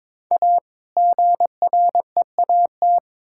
Morse code, heard as a keyed tone, about 22 wpm, 700 Hz. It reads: A GREAT
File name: Build A Phrase - 2-Word Phrases - 22wpm